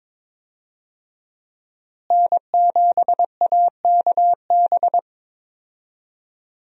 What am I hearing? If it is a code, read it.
Morse code, heard as a keyed tone, about 22 wpm, 700 Hz. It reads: N7AKB
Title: USA Call Signs - 5 Characters - Repeated 2x - 22wpm